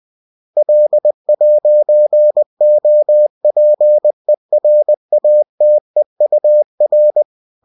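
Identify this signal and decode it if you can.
Morse code, heard as a keyed tone, about 20 wpm, 600 Hz. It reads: L'OPERATEUR
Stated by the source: Bulletin sent in CW.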